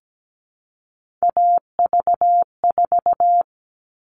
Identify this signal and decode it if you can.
Morse code, heard as a keyed tone, about 17 wpm, 700 Hz. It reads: AV4
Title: All Cut Numbers - 3-Digit 17wpm